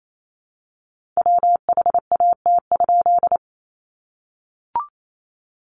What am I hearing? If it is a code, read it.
Morse code, heard as a keyed tone, about 28 wpm, 700 Hz. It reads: WHAT?
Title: A Sound Of Thunder 28wpm